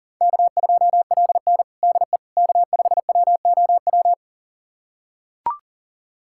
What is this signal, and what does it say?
Morse code, heard as a keyed tone, about 40 wpm, 700 Hz. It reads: K2LN DE K5WOW
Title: USA Callsign DE Callsign 40wpm